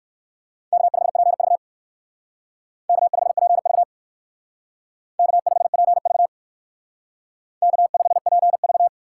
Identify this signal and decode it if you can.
Morse code, heard as a keyed tone, about 59 wpm, 700 Hz. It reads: K5PV K5PV K5PV K5PV